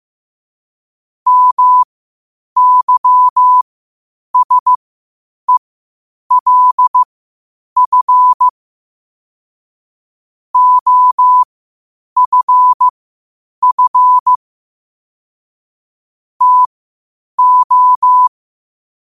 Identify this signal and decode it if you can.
Morse code, heard as a keyed tone, about 15 wpm, spaced as for 8 wpm, 1000 Hz. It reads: MYSELF OFF TO
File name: qotdfiles / 8wpm